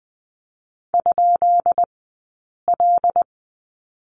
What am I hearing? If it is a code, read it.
Morse code, heard as a keyed tone, about 20 wpm, spaced as for 8 wpm, 700 Hz. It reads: ?L
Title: 165 Lesson 27d - Number 7 - Two Characters 20wpm